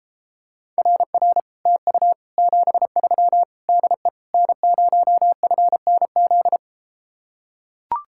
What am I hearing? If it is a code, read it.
Morse code, heard as a keyed tone, about 33 wpm, 700 Hz. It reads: RR TU 73 DE N0FNZ